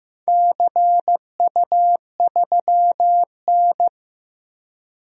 Morse code, heard as a keyed tone, about 15 wpm, 700 Hz. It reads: CU3N
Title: DXCC-100 Call Signs - 4 Characters - 15wpm